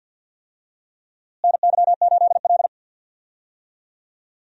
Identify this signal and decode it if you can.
Morse code, heard as a keyed tone, about 51 wpm, 700 Hz. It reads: NY8L
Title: USA Call Signs - 4 Characters - Repeated 2x - 50wpm